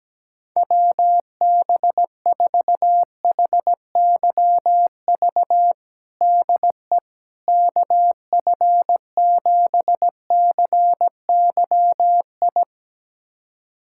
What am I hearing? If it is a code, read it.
Morse code, heard as a keyed tone, about 17 wpm, 700 Hz. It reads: WB4HYV DE KF7CYI
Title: USA Callsign DE Callsign - 6 Characters - 17wpm